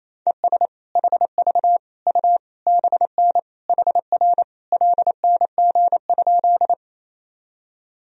Morse code, heard as a keyed tone, about 28 wpm, 700 Hz. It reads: ES HV U BN HR LNG?
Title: Sentences from Top 200 Words and Abbreviations 28wpm